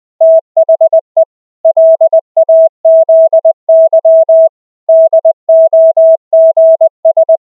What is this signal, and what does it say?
Morse code, heard as a keyed tone, about 20 wpm, 650 Hz. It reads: THE LAZY DOGS
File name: cw / cwfox / REFERENCE-650-Hz